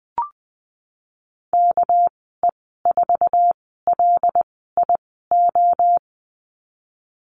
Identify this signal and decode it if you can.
Morse code, heard as a keyed tone, about 20 wpm, 700 Hz. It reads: K E 4 L I O
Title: 238 Lesson 38g - Letter q - Call Signs 20wpm